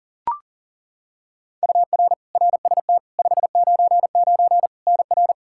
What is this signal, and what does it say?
Morse code, heard as a keyed tone, about 37 wpm, 700 Hz. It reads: UR RST 599 NR
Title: Practice QSO - Exchange 1 - Beginning - 40wpm